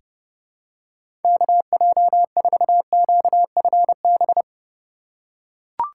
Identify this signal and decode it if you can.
Morse code, heard as a keyed tone, about 30 wpm, 700 Hz. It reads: KJ4QFB
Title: USA Call Signs - 2x Character Spacing - 30wpm